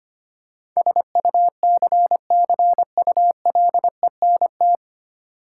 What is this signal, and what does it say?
Morse code, heard as a keyed tone, about 25 wpm, 700 Hz. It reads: SUCCULENT
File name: practice / 25wpm / Difficult Words with Hints - 25wpm